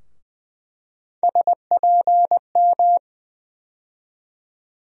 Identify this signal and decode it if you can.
Morse code, heard as a keyed tone, about 20 wpm, 700 Hz. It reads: SPM
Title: USA Call Sign Postfixes - 3 Letters 20wpm